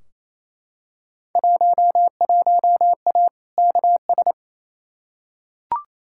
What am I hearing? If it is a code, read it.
Morse code, heard as a keyed tone, about 28 wpm, 700 Hz. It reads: 11A KS
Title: Field Day Exchanges Encore - 28wpm